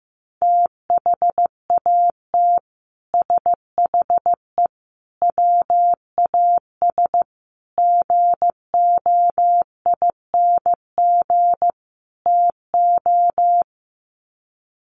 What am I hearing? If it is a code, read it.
Morse code, heard as a keyed tone, about 15 wpm, 700 Hz. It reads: THAT SHE WAS GOING TO
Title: Sets of 5 Words Encore2 15wpm